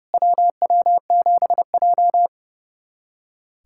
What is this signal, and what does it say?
Morse code, heard as a keyed tone, about 30 wpm, 700 Hz. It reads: WW7J
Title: SOTA QSO Practice 30wpm